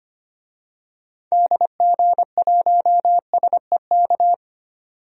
Morse code, heard as a keyed tone, about 25 wpm, 700 Hz. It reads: DG1SEK
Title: DXCC-100 Call Signs - 6 Characters - 25wpm